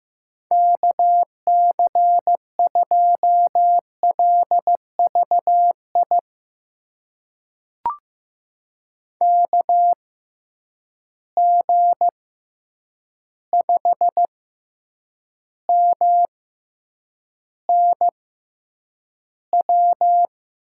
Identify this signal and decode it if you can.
Morse code, heard as a keyed tone, about 15 wpm, spaced as for 8 wpm, 700 Hz. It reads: KC2LVI KG5MNW
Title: USA Call Signs - 6 Characters - 6x Character Spacing - 15wpm